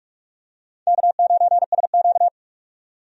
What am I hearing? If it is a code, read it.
Morse code, heard as a keyed tone, about 46 wpm, 700 Hz. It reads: K9SQ
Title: USA Call Signs - 4 Characters - 45wpm